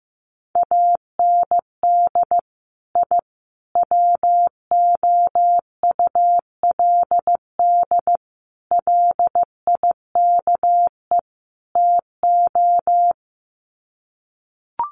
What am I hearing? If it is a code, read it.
Morse code, heard as a keyed tone, about 15 wpm, 700 Hz. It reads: AND I WOULD LIKE TO
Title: Sets of 5 Words Encore 15wpm